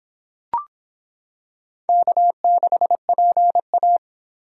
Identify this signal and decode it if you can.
Morse code, heard as a keyed tone, about 26 wpm, 700 Hz. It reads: K6PA